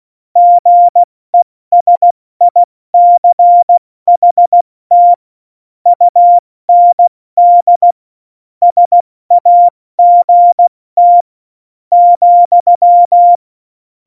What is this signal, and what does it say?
Morse code, heard as a keyed tone, about 16 wpm, spaced as for 13 wpm, 700 Hz. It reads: GESICHT UND SAGT,
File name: Morsekurs / Witze / Witz 15. Mai 2018, 70 F80